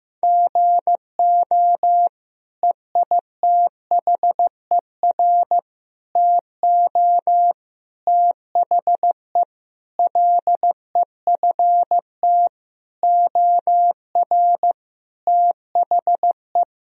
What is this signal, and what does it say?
Morse code, heard as a keyed tone, about 15 wpm, 700 Hz. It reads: GO EITHER TO THE LEFT OR THE
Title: Sentences from Top 1000 Words - Review Part 4 - 15wpm